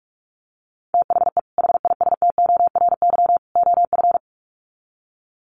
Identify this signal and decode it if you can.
Morse code, heard as a keyed tone, about 44 wpm, 700 Hz. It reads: THE HISTORY OF
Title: Sets of 3 Words 45wpm